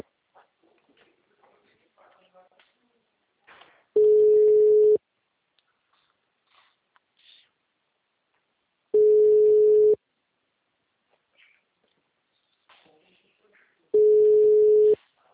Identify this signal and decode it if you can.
Ringback tone (Europe, 425 Hz).
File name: out- -2011- - - .16133